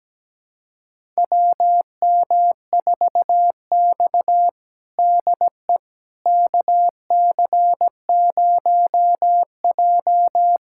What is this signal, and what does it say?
Morse code, heard as a keyed tone, about 17 wpm, 700 Hz. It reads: WM4X DE KC0J
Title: USA Callsign DE Callsign - 4 Characters - 17wpm